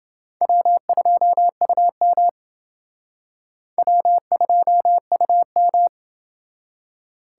Morse code, heard as a keyed tone, about 29 wpm, 700 Hz. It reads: W2UM W2UM